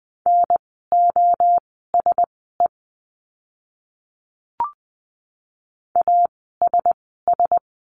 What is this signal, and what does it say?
Morse code, heard as a keyed tone, about 20 wpm, spaced as for 14 wpm, 700 Hz. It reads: NOSE ASS
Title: 036 Lesson 7f - Letter s - Words 20wpm